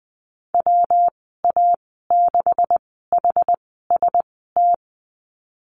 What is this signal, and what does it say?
Morse code, heard as a keyed tone, about 20 wpm, spaced as for 15 wpm, 700 Hz. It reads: WA6HST
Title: 147 Lesson 24g - Letter y - Call Signs 20wpm